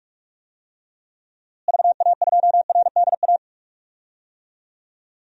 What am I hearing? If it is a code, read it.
Morse code, heard as a keyed tone, about 46 wpm, 700 Hz. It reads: VA2RDA
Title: DXCC-100 Call Signs - 6 Characters - Repeated 3x - 45wpm